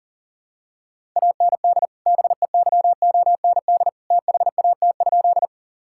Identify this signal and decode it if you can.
Morse code, heard as a keyed tone, about 40 wpm, 700 Hz. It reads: AND BEYOND THAT?
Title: Sentences from Top 1000 Words - Review Part 6 - 40wpm